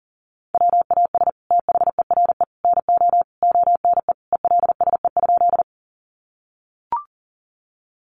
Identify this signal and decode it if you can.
Morse code, heard as a keyed tone, about 40 wpm, 690 Hz. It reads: WAS THERE NO ONE ELSE?
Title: Sentences from Top 800 Words 40wpm